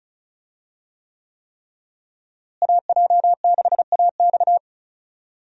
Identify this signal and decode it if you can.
Morse code, heard as a keyed tone, about 34 wpm, 700 Hz. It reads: AJ6AX
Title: USA Call Signs - 5 Characters - Repeated 3x - 35wpm